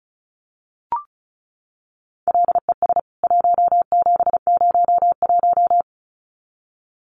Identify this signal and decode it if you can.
Morse code, heard as a keyed tone, about 34 wpm, 700 Hz. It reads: LES 1701
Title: CWT Exchange 35wpm